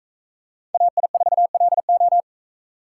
Morse code, heard as a keyed tone, about 40 wpm, 700 Hz. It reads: AI4LO